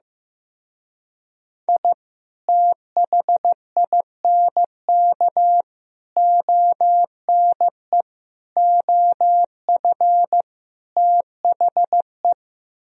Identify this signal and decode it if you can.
Morse code, heard as a keyed tone, about 15 wpm, 700 Hz. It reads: I THINK ONE OF THE
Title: Speed-Racing - Sets of 5 Words 15wpm